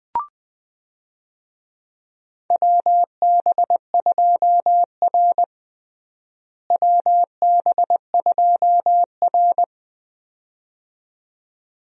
Morse code, heard as a keyed tone, about 21 wpm, 700 Hz. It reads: WB2R WB2R
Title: USA Call Signs - 4 Characters - Repeated 2x - 20wpm